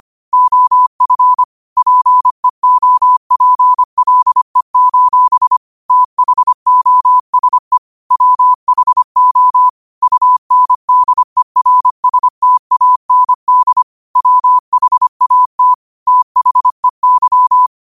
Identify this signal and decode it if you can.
Morse code, heard as a keyed tone, about 25 wpm, 1000 Hz. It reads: OF PEOPLE: THOSE WHO UNDERSTAND WHAT THEY
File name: qotdfiles / 25wpm